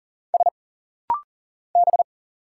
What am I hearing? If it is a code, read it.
Morse code, heard as a keyed tone, about 40 wpm, 700 Hz. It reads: S B
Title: Single Letters - Rapid-Fire 40wpm